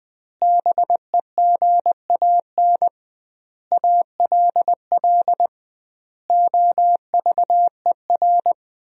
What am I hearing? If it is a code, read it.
Morse code, heard as a keyed tone, about 20 wpm, 700 Hz. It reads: BEGAN ALL OVER
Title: 2x Word-Spacing - Sentences from Top 500 Words - 20wpm